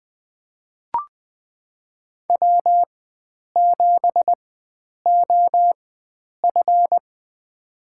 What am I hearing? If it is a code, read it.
Morse code, heard as a keyed tone, about 20 wpm, spaced as for 9 wpm, 700 Hz. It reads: W7OF
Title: USA Call Signs - 4 Characters - 4x Character Spacing - 20wpm